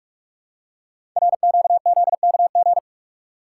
Morse code, heard as a keyed tone, about 44 wpm, 700 Hz. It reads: RQZKG
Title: Code Groups - 5 Letter:Numbers - 45wpm